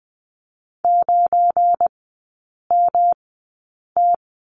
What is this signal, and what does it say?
Morse code, heard as a keyed tone, about 20 wpm, spaced as for 8 wpm, 700 Hz. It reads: 9MT
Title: 172 Lesson 28e - Number 9 - Three Characters 20wpm